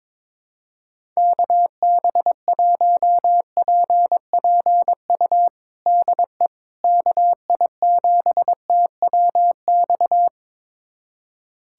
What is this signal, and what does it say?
Morse code, heard as a keyed tone, about 22 wpm, 700 Hz. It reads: KB1PPU DE KI7TWX
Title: USA Callsign DE Callsign - 6 Characters - 22wpm